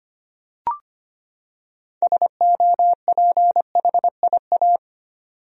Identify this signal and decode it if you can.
Morse code, heard as a keyed tone, about 25 wpm, 700 Hz. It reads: SOPHIA
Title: USA Common Names - 25wpm